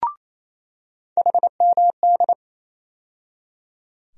Morse code, heard as a keyed tone, about 28 wpm, 700 Hz. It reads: HMD
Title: USA Call Sign Postfixes - 3 Letters 28wpm